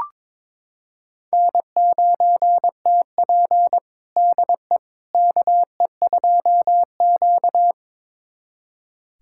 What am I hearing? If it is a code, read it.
Morse code, heard as a keyed tone, about 21 wpm, 700 Hz. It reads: N9TP DE KE2Q
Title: USA Callsign DE Callsign - 4 Characters - 22wpm